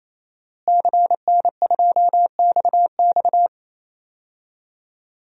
Morse code, heard as a keyed tone, about 28 wpm, 700 Hz. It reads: CN2XX
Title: DXCC-100 Call Signs - 5 Characters - Repeated 3x - 28wpm